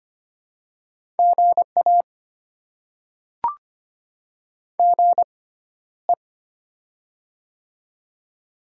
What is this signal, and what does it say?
Morse code, heard as a keyed tone, about 25 wpm, spaced as for 13 wpm, 700 Hz. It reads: GA GE
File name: Top 20 QSO Elements - 6x Character Spacing - 25wpm